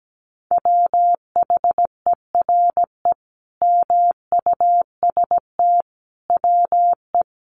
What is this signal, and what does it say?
Morse code, heard as a keyed tone, about 17 wpm, 700 Hz. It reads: WHERE MUST WE
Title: Sentences from Top 1300 Words 17wpm